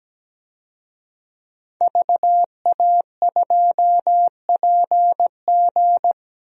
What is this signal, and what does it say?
Morse code, heard as a keyed tone, about 17 wpm, 700 Hz. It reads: VA2PG
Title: DXCC-100 Call Signs - 5 Characters - Repeated 3x - 17wpm